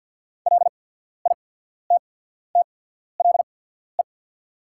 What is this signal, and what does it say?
Morse code, heard as a keyed tone, about 48 wpm, spaced as for 14 wpm, 690 Hz. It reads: LITTLE
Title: 8x Character Spacing - Top 200 Words - 50wpm